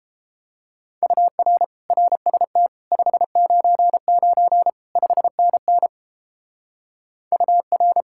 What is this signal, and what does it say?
Morse code, heard as a keyed tone, about 31 wpm, 700 Hz. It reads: UR RST 599 5NN UR